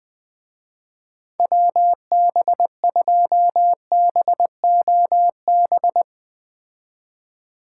Morse code, heard as a keyed tone, about 21 wpm, 700 Hz. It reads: WB2BOB
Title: Speed Racing - USA 2x3 Call Signs 17wpm